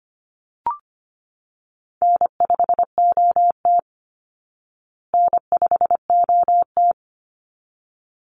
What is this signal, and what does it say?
Morse code, heard as a keyed tone, about 25 wpm, 700 Hz. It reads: N5OT N5OT
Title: Easy Call Signs Repeated 2x - 25wpm